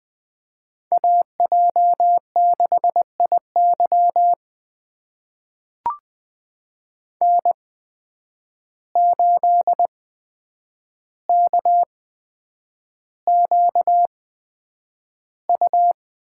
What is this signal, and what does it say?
Morse code, heard as a keyed tone, about 20 wpm, spaced as for 9 wpm, 700 Hz. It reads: AJ6IY N8KQU
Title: USA Call Signs - 5 Characters - 8x Character Spacing - 20wpm